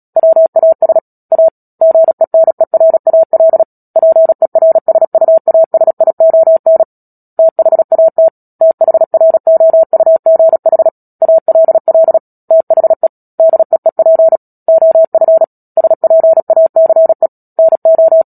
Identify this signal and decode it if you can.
Morse code, heard as a keyed tone, about 37 wpm, 650 Hz. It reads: WAS A GENERAL PERSUASION THAT THROUGH ALL THE DEEP OF SPACE NO
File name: Book Two / 10-Chapter 10 - Epilogue - 36 WPM